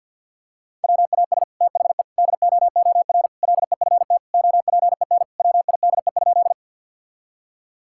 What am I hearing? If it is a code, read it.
Morse code, heard as a keyed tone, about 52 wpm, 700 Hz. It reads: WAS THE DOOR LEFT OPEN WIDE?
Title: Sentences from Top 1000 Words - Review Part 6 - 50wpm